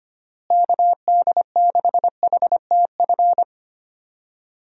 Morse code, heard as a keyed tone, about 25 wpm, 700 Hz. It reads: KD6HTF
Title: USA Call Signs - 6 Characters - Repeated 2x - 25wpm